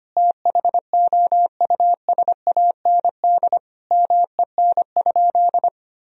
Morse code, heard as a keyed tone, about 25 wpm, 700 Hz. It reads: THOUSAND MEN?
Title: Sentences from Top 1000 Words - Review Part 3 - 25wpm